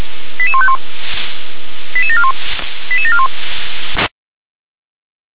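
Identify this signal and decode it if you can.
Five-tone selective call (ZVEI), heard as quick runs of five tones, tones of about 70 ms. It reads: 88151 88741 88741
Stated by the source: Received on SDR.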